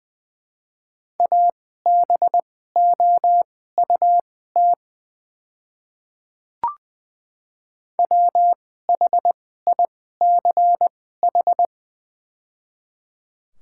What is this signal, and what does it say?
Morse code, heard as a keyed tone, about 20 wpm, spaced as for 15 wpm, 700 Hz. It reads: ABOUT WHICH
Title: 2x Character Spacing - Top 200 Words - 20wpm